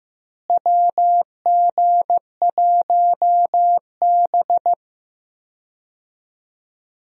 Morse code, heard as a keyed tone, about 15 wpm, 700 Hz. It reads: WG1B
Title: USA Call Signs - 4 Characters - Repeated 2x - 15wpm